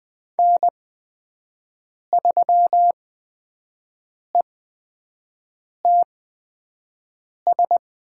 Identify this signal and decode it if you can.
Morse code, heard as a keyed tone, about 20 wpm, spaced as for 5 wpm, 700 Hz. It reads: N3ETS
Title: USA Call Signs - 5 Characters - 8x Character Spacing - 20wpm